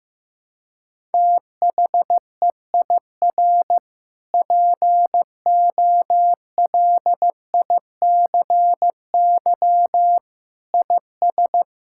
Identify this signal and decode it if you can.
Morse code, heard as a keyed tone, about 16 wpm, 700 Hz. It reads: THEIR POLICY IS
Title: Sentences from Top 1000 Words - Review Part 5 - 15wpm